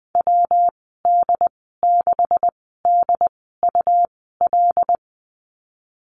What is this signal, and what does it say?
Morse code, heard as a keyed tone, about 20 wpm, spaced as for 14 wpm, 700 Hz. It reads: WD6DUL
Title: 128 Lesson 21g - Number 6 - Call Signs 20wpm